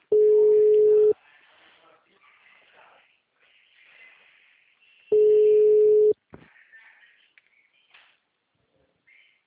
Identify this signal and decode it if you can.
Ringback tone (Europe, 425 Hz).